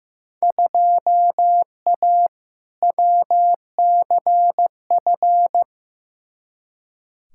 Morse code, heard as a keyed tone, about 15 wpm, 700 Hz. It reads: 2A WCF
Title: Field Day Exchanges - 15wpm